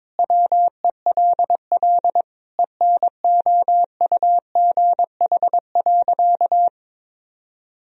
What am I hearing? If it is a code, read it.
Morse code, heard as a keyed tone, about 22 wpm, 700 Hz. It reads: WELL ENOUGH.
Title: Sentences from Top 400 Words and Abbreviations 22wpm